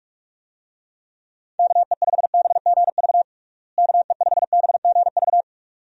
Morse code, heard as a keyed tone, about 46 wpm, 700 Hz. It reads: KE5BGV KE5BGV
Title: USA Call Signs - 6 Characters - Repeated 2x - 45wpm